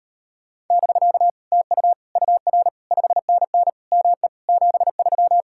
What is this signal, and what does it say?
Morse code, heard as a keyed tone, about 36 wpm, 700 Hz. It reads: <BK> TU UR 5NN ME 73